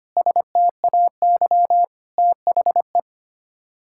Morse code, heard as a keyed tone, about 25 wpm, 700 Hz. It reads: STAY THE
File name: Sets of 9 Words - 25wpm